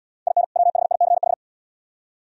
Morse code, heard as a keyed tone, about 78 wpm, 700 Hz. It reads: IN WHICH